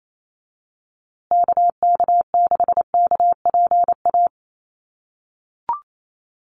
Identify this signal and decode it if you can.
Morse code, heard as a keyed tone, about 28 wpm, 700 Hz. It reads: KK6KPA
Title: USA Call Signs Repeated 3x 28wpm